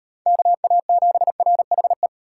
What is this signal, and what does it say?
Morse code, heard as a keyed tone, about 35 wpm, 700 Hz. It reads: KA7RHE